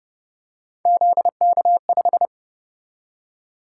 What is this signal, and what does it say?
Morse code, heard as a keyed tone, about 31 wpm, 700 Hz. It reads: ZK5